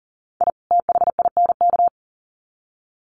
Morse code, heard as a keyed tone, about 40 wpm, 700 Hz. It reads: I THINK
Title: Sets of 2 Words 40wpm